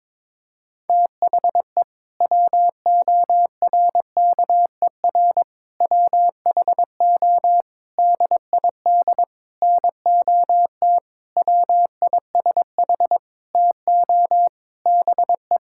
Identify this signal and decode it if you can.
Morse code, heard as a keyed tone, about 22 wpm, 700 Hz. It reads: THE WORKER WHO DID NOT WISH TO BE
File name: Sentences with 7-Word Phrases - 22wpm